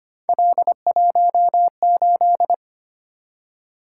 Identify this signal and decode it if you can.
Morse code, heard as a keyed tone, about 25 wpm, 700 Hz. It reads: L18
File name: Three Letter-Number Combinations - 25wpm